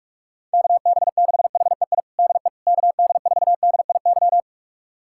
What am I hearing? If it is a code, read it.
Morse code, heard as a keyed tone, about 46 wpm, 700 Hz. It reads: KB6HEI DE KD4DIY
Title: USA Callsign DE Callsign - 6 Characters - 45wpm